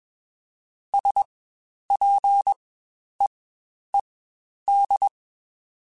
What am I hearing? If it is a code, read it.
Morse code, heard as a keyed tone, about 21 wpm, spaced as for 10 wpm, 800 Hz. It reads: SPEED
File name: Spaced Characters / Spaced Characters-mp3s / Weather 5 12 sp @20